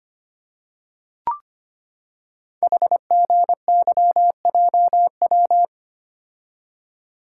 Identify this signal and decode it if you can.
Morse code, heard as a keyed tone, about 25 wpm, 690 Hz. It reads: HGYJW
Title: Code Groups - 5 Letters - 25wpm